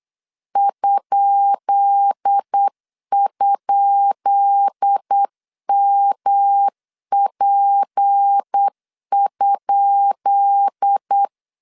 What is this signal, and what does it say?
Morse code, heard as a keyed tone, about 8 wpm, 800 Hz. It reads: ??MP?
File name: LEZIONI / 10 SIGNS / 10B INT 8wpm 3534